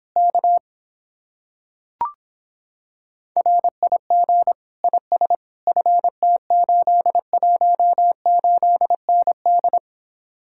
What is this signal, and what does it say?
Morse code, heard as a keyed tone, about 26 wpm, 700 Hz. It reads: K RIG IS FT818ND